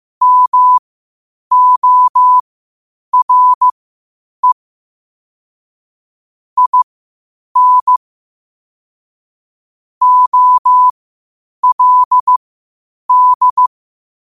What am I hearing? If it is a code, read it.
Morse code, heard as a keyed tone, about 15 wpm, spaced as for 8 wpm, 1000 Hz. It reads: MORE IN OLD
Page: qotd-at-8-wpm